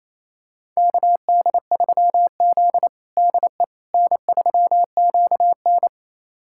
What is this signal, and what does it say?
Morse code, heard as a keyed tone, about 28 wpm, 700 Hz. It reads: KD3Z DE N3QN
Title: USA Callsign DE Callsign - 4 Characters - 28wpm